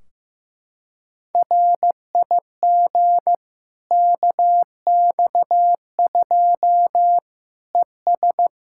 Morse code, heard as a keyed tone, about 15 wpm, 700 Hz. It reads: RIG KX2 ES
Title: Practice QSO - Exchange 2 - Info - 15wpm